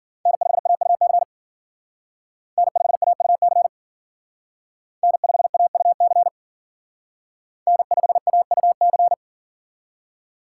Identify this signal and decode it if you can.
Morse code, heard as a keyed tone, about 50 wpm, 700 Hz. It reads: N5AUC N5AUC N5AUC N5AUC
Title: Speed Racing - USA Call Signs 40wpm